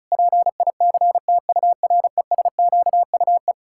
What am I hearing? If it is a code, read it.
Morse code, heard as a keyed tone, about 36 wpm, 700 Hz. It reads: PICTURESQUE
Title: Pangram Challenge v2 - 35wpm